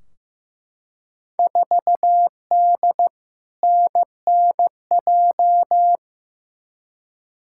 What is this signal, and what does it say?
Morse code, heard as a keyed tone, about 15 wpm, 700 Hz. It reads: 4D NNJ
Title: Field Day Exchanges Encore - 15wpm